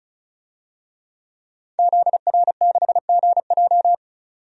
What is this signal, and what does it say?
Morse code, heard as a keyed tone, about 36 wpm, 700 Hz. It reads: ZR6GJ
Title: DXCC-100 Call Signs - 5 Characters - 35wpm